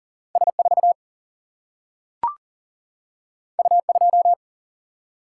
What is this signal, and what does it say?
Morse code, heard as a keyed tone, about 40 wpm, 700 Hz. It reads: S4 U2
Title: Single-Letter Single-Number 40wpm